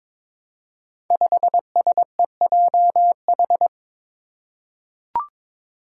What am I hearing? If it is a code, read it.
Morse code, heard as a keyed tone, about 22 wpm, 700 Hz. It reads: 5SEJH
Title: Five Challenging Letter-Number Combinations 22wpm